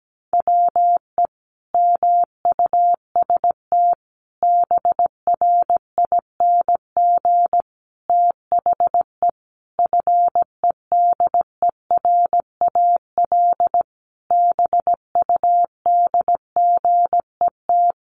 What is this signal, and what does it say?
Morse code, heard as a keyed tone, about 17 wpm, 700 Hz. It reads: WE MUST BRING THE FEDERAL BUDGET
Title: Sentences from Top 1200 Words 17wpm